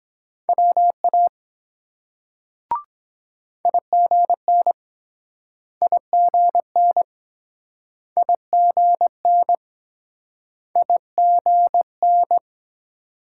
Morse code, heard as a keyed tone, about 22 wpm, 700 Hz. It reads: WA IGN IGN IGN IGN